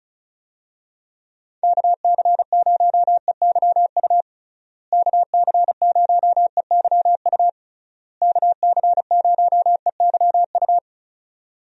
Morse code, heard as a keyed tone, about 34 wpm, 700 Hz. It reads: KC0EYU KC0EYU KC0EYU
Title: USA Call Signs - 6 Characters - Repeated 3x - 35wpm